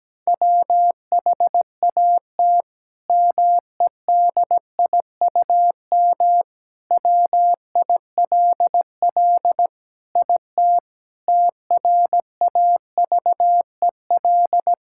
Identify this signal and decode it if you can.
Morse code, heard as a keyed tone, about 17 wpm, 700 Hz. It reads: WHAT MEDIUM WILL IT TRAVEL
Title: Sentences from Top 1000 Words - Review Part 6 - 17wpm